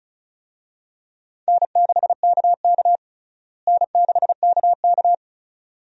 Morse code, heard as a keyed tone, about 36 wpm, 700 Hz. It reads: N6KK N6KK
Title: USA Call Signs - 4 Characters - Repeated 2x - 35wpm